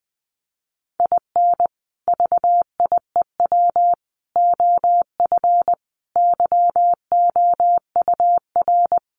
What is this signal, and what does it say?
Morse code, heard as a keyed tone, about 20 wpm, 700 Hz. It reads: IN VIEW OF YOUR
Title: Sentences from Top 900 Words 20wpm